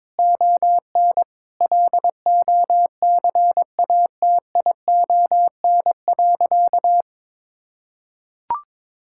Morse code, heard as a keyed tone, about 22 wpm, 700 Hz. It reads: ON LOCATION.
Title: Sentences from Top 1000 Words - Review Part 6 - 22wpm